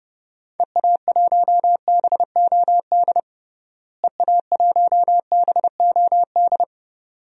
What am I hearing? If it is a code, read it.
Morse code, heard as a keyed tone, about 29 wpm, 700 Hz. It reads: EA1BOD EA1BOD
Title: DXCC-100 Call Signs - 6 Characters - Repeated 2x - 30wpm